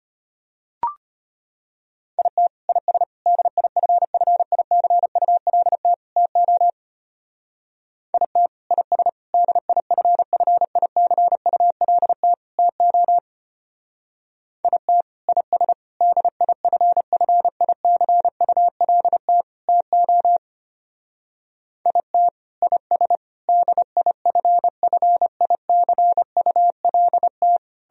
Morse code, heard as a keyed tone, about 32 wpm, 700 Hz. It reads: IT IS DIFFICULT TO IT IS DIFFICULT TO IT IS DIFFICULT TO IT IS DIFFICULT